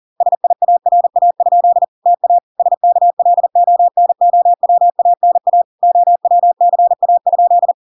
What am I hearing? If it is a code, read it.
Morse code, heard as a keyed tone, about 40 wpm, 700 Hz. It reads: SIARA? TA SKLONOWANA OWCA?